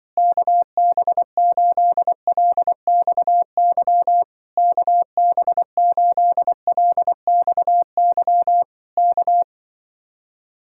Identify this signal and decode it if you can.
Morse code, heard as a keyed tone, about 24 wpm, 700 Hz. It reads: KB8LXY KB8LXY K
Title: Speed-Racing Common QSO Phrases - 20wpm